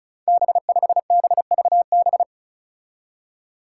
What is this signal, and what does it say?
Morse code, heard as a keyed tone, about 36 wpm, 700 Hz. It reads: B5BVB
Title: Five Challenging Letter-Number Combinations 35wpm